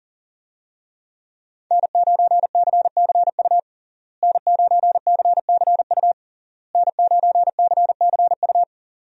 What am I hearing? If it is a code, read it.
Morse code, heard as a keyed tone, about 40 wpm, 700 Hz. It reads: N9CCU N9CCU N9CCU
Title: USA Call Signs - 5 Characters - Repeated 3x - 40wpm